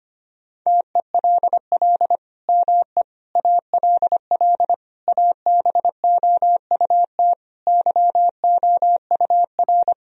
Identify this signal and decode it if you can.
Morse code, heard as a keyed tone, about 25 wpm, 700 Hz. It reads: TELL ME ALL ABOUT YOUR
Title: Sentences from Top 1000 Words - Review Part 4 - 25wpm